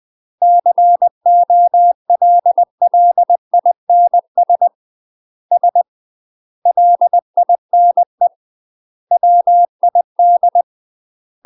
Morse code, heard as a keyed tone, about 20 wpm, 700 Hz. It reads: COLLINS S LINE WID
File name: UK MASTER QSOs TRAINING / 20WPM / UK MASTER QSOs 0029